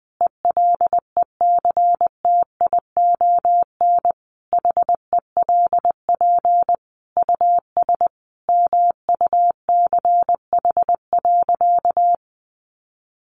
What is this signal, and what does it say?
Morse code, heard as a keyed tone, about 20 wpm, 700 Hz. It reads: ELECTION HELP US MUCH.
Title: Sentences from Top 1500 Words 20wpm